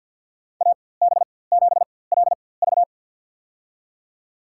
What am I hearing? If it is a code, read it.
Morse code, heard as a keyed tone, about 50 wpm, spaced as for 23 wpm, 700 Hz. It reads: AB7LV
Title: USA Call Signs - 4x Character Spacing - 50wpm